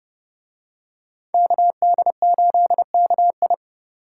Morse code, heard as a keyed tone, about 31 wpm, 700 Hz. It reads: KD8KI